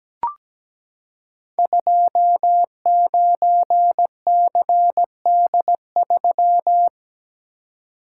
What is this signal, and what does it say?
Morse code, heard as a keyed tone, about 17 wpm, 700 Hz. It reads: 29CD3
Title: Code Groups - 5 Letter:Numbers - 17wpm